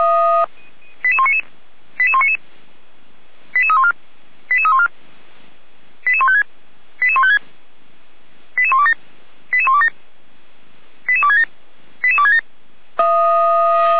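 Five-tone selective call (ZVEI), heard as quick runs of five tones, tones of about 70 ms. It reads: 88199 88199 88314 88314 88156 88156 88127 88127 88267 88267
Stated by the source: Five-tone ZVEI selective call received on an SDR.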